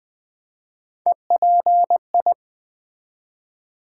Morse code, heard as a keyed tone, about 20 wpm, 700 Hz. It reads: EPI
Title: Build A Word - 3 Letters At A Time - 8-Letter Words - 20wpm